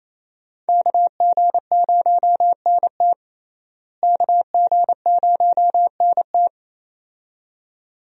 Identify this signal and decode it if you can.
Morse code, heard as a keyed tone, about 28 wpm, 700 Hz. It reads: KG0NT KG0NT
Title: USA Call Signs - 5 Characters - Repeated 2x - 28wpm